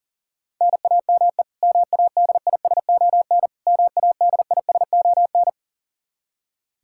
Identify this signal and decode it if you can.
Morse code, heard as a keyed tone, about 37 wpm, 700 Hz. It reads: NAME MADISON MADISON
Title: K1USN SST QSO Practice 40wpm